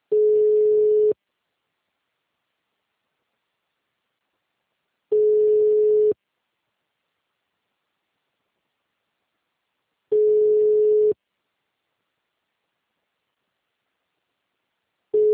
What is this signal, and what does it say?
Ringback tone (Europe, 425 Hz).
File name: out- -2029- - - .4495